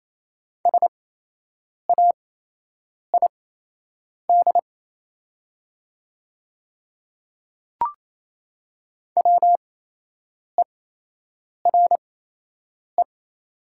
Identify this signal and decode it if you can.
Morse code, heard as a keyed tone, about 28 wpm, spaced as for 8 wpm, 700 Hz. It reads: SAID WERE
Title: 8x Character Spacing - Top 200 Words - 28wpm